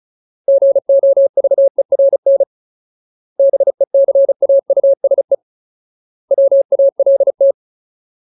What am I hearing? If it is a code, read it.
Morse code, heard as a keyed tone, about 35 wpm, 550 Hz. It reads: GOVERN BECAUSE WALT
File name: Ultimate Word List / 35WPM Ultimate